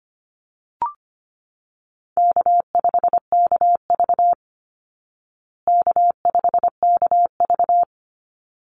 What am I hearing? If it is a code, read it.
Morse code, heard as a keyed tone, about 25 wpm, 700 Hz. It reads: K5KV K5KV
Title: Easy Call Signs Repeated 2x - 25wpm